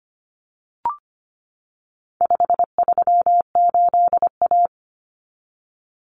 Morse code, heard as a keyed tone, about 25 wpm, 700 Hz. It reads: 538A
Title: Common Cut Numbers - 4-Digit 25wpm